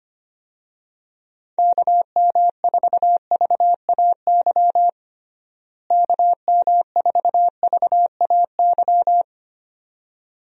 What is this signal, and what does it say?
Morse code, heard as a keyed tone, about 25 wpm, 700 Hz. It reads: KM4VAY KM4VAY
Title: USA Call Signs - 6 Characters - Repeated 2x - 25wpm